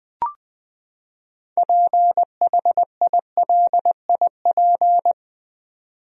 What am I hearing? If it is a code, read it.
Morse code, heard as a keyed tone, about 20 wpm, 700 Hz. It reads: PHILIP
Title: USA Common Names - 20wpm